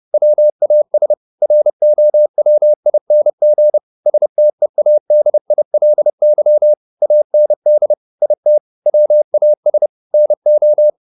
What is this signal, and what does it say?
Morse code, heard as a keyed tone, about 30 wpm, 600 Hz. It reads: WAS ROWING STEADILY AND IT WAS NO
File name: the old man and the sea / the old man and the sea 1 30wpm